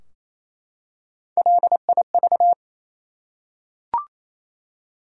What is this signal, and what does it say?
Morse code, heard as a keyed tone, about 28 wpm, 700 Hz. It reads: LIV
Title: USA Common Names - Max 3 Letters - 28wpm